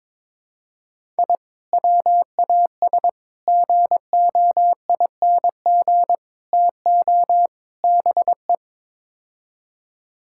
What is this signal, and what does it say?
Morse code, heard as a keyed tone, about 22 wpm, 700 Hz. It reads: I WAS GOING TO BE
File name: practice / 22wpm / Build A Phrase - 5-Word Phrases - 22wpm